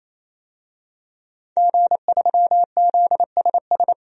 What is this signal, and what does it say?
Morse code, heard as a keyed tone, about 28 wpm, 700 Hz. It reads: G3ZSS
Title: DXCC-100 Call Signs - 5 Characters - 28wpm